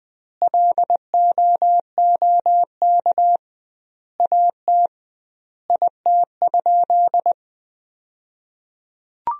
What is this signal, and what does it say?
Morse code, heard as a keyed tone, about 21 wpm, 700 Hz. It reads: LOOK AT IT?
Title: 2x Word-Spacing - Sentences from Top 100 Words - 20wpm